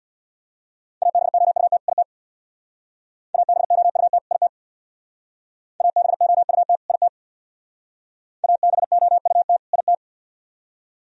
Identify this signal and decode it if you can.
Morse code, heard as a keyed tone, about 64 wpm, 700 Hz. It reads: ABOUT IT ABOUT IT ABOUT IT ABOUT IT